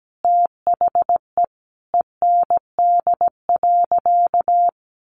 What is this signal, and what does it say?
Morse code, heard as a keyed tone, about 17 wpm, 700 Hz. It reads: THE END.
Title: Sentences from Top 1300 Words 17wpm